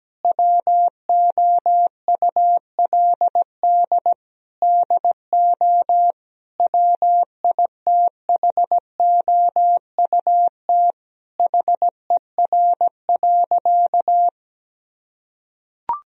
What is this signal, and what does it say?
Morse code, heard as a keyed tone, about 17 wpm, 700 Hz. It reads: WOULD DO WITHOUT HER.
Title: Sentences from Top 1000 Words - Review Part 3 - 17wpm